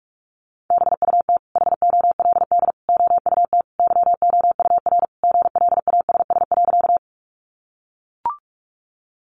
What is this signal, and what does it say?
Morse code, heard as a keyed tone, about 44 wpm, 700 Hz. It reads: BUT HOLD OUT YOUR GLASS.
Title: Sentences from Top 1300 Words 45wpm